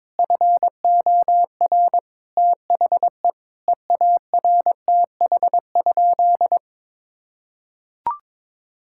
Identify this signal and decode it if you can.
Morse code, heard as a keyed tone, about 22 wpm, 700 Hz. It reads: FOR THE EARTH?
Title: Sentences from Top 1000 Words - Review Part 5 - 22wpm